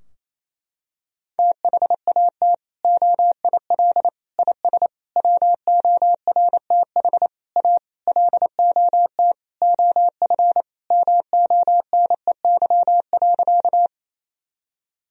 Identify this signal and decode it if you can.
Morse code, heard as a keyed tone, about 28 wpm, 700 Hz. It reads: THAT OIL IS WORTH A LOT OF MONEY.
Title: Sentences from Top 1000 Words - Review Part 4 - 28wpm